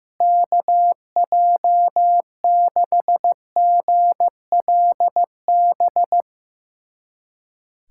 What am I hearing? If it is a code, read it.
Morse code, heard as a keyed tone, about 15 wpm, 700 Hz. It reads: KJ6GLB
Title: USA Callsign DE Callsign - 6 Characters - 15wpm